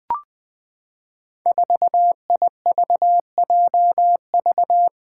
Morse code, heard as a keyed tone, about 20 wpm, 700 Hz. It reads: 4IVJV
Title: Five Challenging Letter-Number Combinations 20wpm